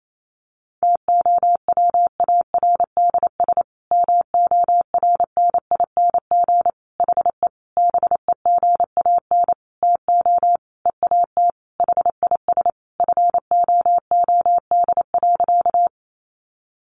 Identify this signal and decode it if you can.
Morse code, heard as a keyed tone, about 28 wpm, 700 Hz. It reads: TOWARDS MORNING HE BEGAN TO EAT HIS FOOD.
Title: Sentences from Top 1100 Words 28wpm